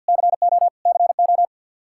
Morse code, heard as a keyed tone, about 52 wpm, 700 Hz. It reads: CO CO
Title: K1USN SST QSO Practice 50wpm